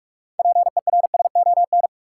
Morse code, heard as a keyed tone, about 42 wpm, 700 Hz. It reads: PERSON
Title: Six Letter Words - 45wpm